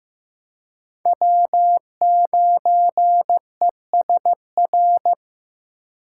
Morse code, heard as a keyed tone, about 15 wpm, 700 Hz. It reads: W9ESR
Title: USA Call Signs - 5 Characters - 4x Character Spacing - 15wpm